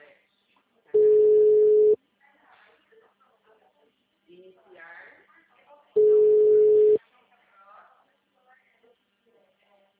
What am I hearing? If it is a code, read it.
Ringback tone (Europe, 425 Hz), with speech.
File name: out- -2049- - - .7987